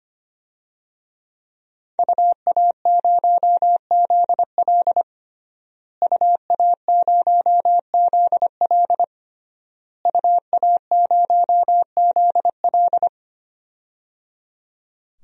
Morse code, heard as a keyed tone, about 25 wpm, 700 Hz. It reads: UA0ZL UA0ZL UA0ZL
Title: DXCC-100 Call Signs - 5 Characters - Repeated 3x - 25wpm